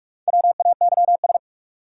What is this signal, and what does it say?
Morse code, heard as a keyed tone, about 46 wpm, 700 Hz. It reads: WAYS
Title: Sets of 5 Words Encore - New Word Drill - 45wpm